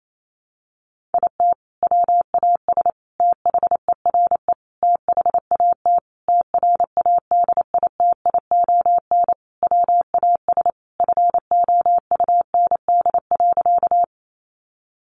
Morse code, heard as a keyed tone, about 28 wpm, 700 Hz. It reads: IT WAS THERE THAT TRADITION WAS FOUND.
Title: Sentences from Top 1500 Words 28wpm